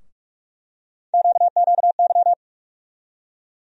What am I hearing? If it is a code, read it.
Morse code, heard as a keyed tone, about 42 wpm, 700 Hz. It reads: QQY
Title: USA Call Sign Postfixes - 3 Letters 45wpm